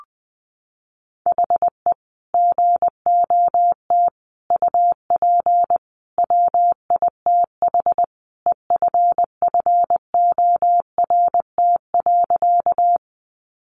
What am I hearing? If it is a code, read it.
Morse code, heard as a keyed tone, about 20 wpm, 700 Hz. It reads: HE GOT UP WITH EFFORT.
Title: Sentences from Top 1200 Words 20wpm